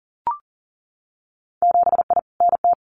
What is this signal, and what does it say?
Morse code, heard as a keyed tone, about 41 wpm, 700 Hz. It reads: 7I NT
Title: Winter Field Day Exchanges Encore 40wpm